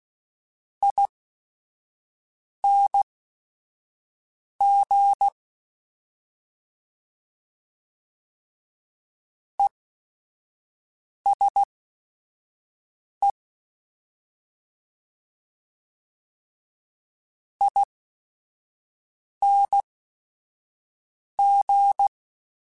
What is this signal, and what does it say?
Morse code, heard as a keyed tone, about 16 wpm, spaced as for 5 wpm, 800 Hz. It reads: ING ESE ING
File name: Short Runs / Short Runs-mp3s / ESE ING @5